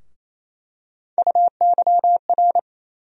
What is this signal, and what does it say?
Morse code, heard as a keyed tone, about 28 wpm, 700 Hz. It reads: UYR
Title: USA Call Sign Postfixes - 3 Letters 28wpm